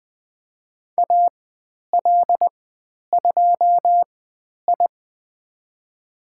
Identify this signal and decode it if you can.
Morse code, heard as a keyed tone, about 20 wpm, spaced as for 10 wpm, 700 Hz. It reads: AL2I
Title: 102 Lesson 17g - Letter u - Call Signs 20wpm